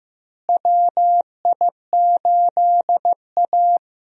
Morse code, heard as a keyed tone, about 15 wpm, 700 Hz. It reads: WI8A
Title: USA Call Signs - 4 Characters - Repeated 3x - 15wpm